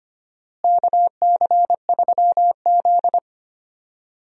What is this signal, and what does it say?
Morse code, heard as a keyed tone, about 25 wpm, 700 Hz. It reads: KC3Z
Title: USA Call Signs - 4 Characters - Repeated 3x - 25wpm